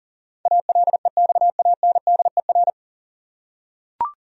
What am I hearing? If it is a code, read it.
Morse code, heard as a keyed tone, about 40 wpm, 700 Hz. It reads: ALEXANDER
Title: USA Common Names - 40wpm